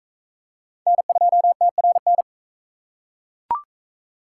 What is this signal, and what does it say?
Morse code, heard as a keyed tone, about 40 wpm, 700 Hz. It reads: N2TRN